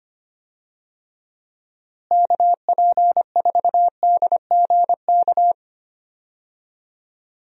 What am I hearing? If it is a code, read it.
Morse code, heard as a keyed tone, about 25 wpm, 700 Hz. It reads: KP4DGK
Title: USA Call Signs - 6 Characters - Repeated 2x - 25wpm